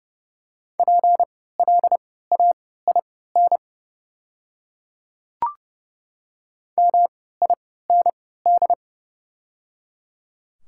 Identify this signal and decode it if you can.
Morse code, heard as a keyed tone, about 30 wpm, spaced as for 17 wpm, 700 Hz. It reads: PLAIN MIND
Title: 3x Character Spacing - Top 500 Words - 30wpm